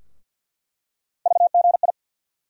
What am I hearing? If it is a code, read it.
Morse code, heard as a keyed tone, about 52 wpm, 700 Hz. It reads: VGI
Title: USA Call Sign Postfixes - 3 Letters 50wpm